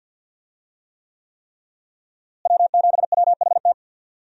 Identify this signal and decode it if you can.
Morse code, heard as a keyed tone, about 45 wpm, 700 Hz. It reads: W7WHT
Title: USA Call Signs - 5 Characters - Repeated 3x - 50wpm